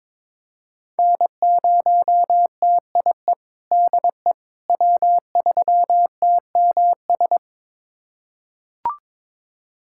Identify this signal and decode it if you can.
Morse code, heard as a keyed tone, about 22 wpm, 700 Hz. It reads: N0TIE DE W3TMS
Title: USA Callsign DE Callsign - 5 Characters - 22wpm